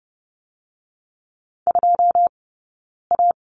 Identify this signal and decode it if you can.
Morse code, heard as a keyed tone, about 30 wpm, 700 Hz. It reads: 2 A
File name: Lesson 20d - Number 3 - Two Characters 30wpm